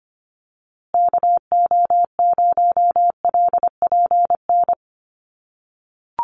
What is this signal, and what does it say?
Morse code, heard as a keyed tone, about 25 wpm, 700 Hz. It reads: KO0LPN
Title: USA Call Signs Repeated 3x 25wpm